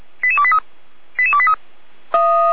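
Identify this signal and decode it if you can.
Five-tone selective call (ZVEI), heard as quick runs of five tones, tones of about 70 ms. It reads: 88283 88283
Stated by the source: ZVEI five-tone selective call 88499 received on an SDR, recorded by feuerwehr-orlamuende.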